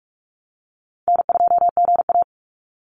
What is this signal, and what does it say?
Morse code, heard as a keyed tone, about 44 wpm, 700 Hz. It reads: N2GA
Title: Call Signs - 45wpm